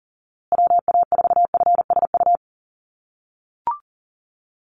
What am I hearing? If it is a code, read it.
Morse code, heard as a keyed tone, about 40 wpm, 700 Hz. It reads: WA4FSU
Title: USA Call Signs Repeated 2x 40wpm